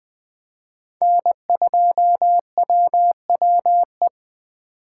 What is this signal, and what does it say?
Morse code, heard as a keyed tone, about 20 wpm, 700 Hz. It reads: N2WWE
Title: USA Call Signs - 5 Characters - 3x Character Spacing - 20wpm